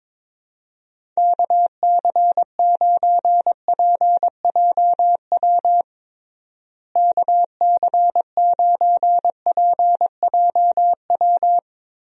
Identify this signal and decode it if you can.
Morse code, heard as a keyed tone, about 22 wpm, 700 Hz. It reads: KC9PJW KC9PJW
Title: USA Call Signs - 6 Characters - Repeated 3x - 22wpm